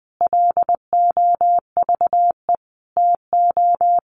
Morse code, heard as a keyed tone, about 20 wpm, 700 Hz. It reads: LOVE TO
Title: Sentences from Top 1400 Words 20wpm